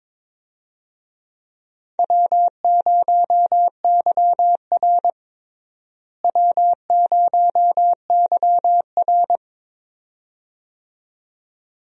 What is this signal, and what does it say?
Morse code, heard as a keyed tone, about 22 wpm, 700 Hz. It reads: W0YR W0YR
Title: USA Call Signs - 4 Characters - Repeated 2x - 22wpm